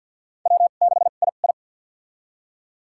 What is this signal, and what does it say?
Morse code, heard as a keyed tone, about 50 wpm, spaced as for 36 wpm, 700 Hz. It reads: W6II
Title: USA Call Signs - 4 Characters - 2x Character Spacing - 50wpm